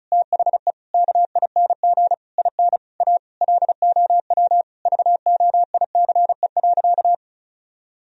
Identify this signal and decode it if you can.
Morse code, heard as a keyed tone, about 34 wpm, 700 Hz. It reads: THE KING IN A LOW VOICE.
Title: Sentences from Top 1000 Words - Review Part 1 - 35wpm